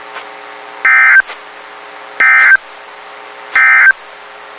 Emergency Alert System (SAME) tones, heard as harsh buzzing bursts. It is heard from 0.7 to 4.0 s.
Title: WXK91 NOAA Weather Radio